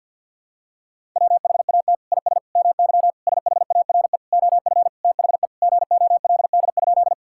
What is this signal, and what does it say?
Morse code, heard as a keyed tone, about 50 wpm, 700 Hz. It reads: WHAT IS MY SHARE OF THE GOLD?
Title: Sentences from Top 1000 Words - Review Part 6 - 50wpm